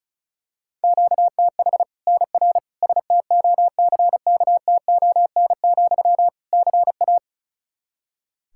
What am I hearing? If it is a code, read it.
Morse code, heard as a keyed tone, about 34 wpm, 700 Hz. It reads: QTH NR STOCKTON, CA
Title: Practice QSO - Exchange 1 - Beginning - 35wpm